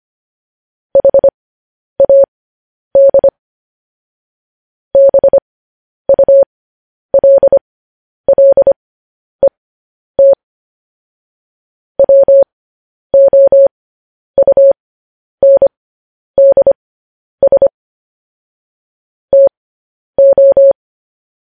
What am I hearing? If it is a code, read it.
Morse code, heard as a keyed tone, about 25 wpm, spaced as for 10 wpm, 550 Hz. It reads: HAD BULLET WOUNDS TO